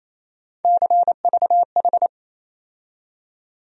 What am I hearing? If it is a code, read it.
Morse code, heard as a keyed tone, about 28 wpm, 700 Hz. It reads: CVH
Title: USA Call Sign Postfixes - 3 Letters 28wpm